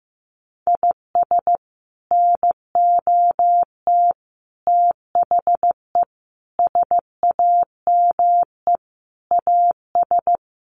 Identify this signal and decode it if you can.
Morse code, heard as a keyed tone, about 15 wpm, 700 Hz. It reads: IS NOT THE SAME AS
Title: Sets of 5 Words Encore 15wpm